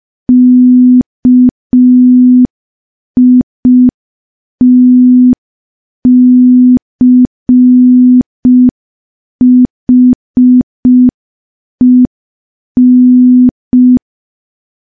Morse code, heard as a keyed tone, about 5 wpm, 250 Hz. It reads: KITCHEN
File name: morse / morse-training-05WPM